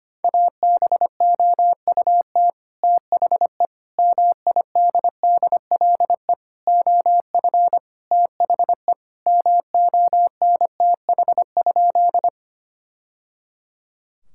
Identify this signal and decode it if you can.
Morse code, heard as a keyed tone, about 25 wpm, 700 Hz. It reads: ABOUT THE MIDDLE OF THE MONTH?
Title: Sentences from Top 1000 Words - Review Part 6 - 25wpm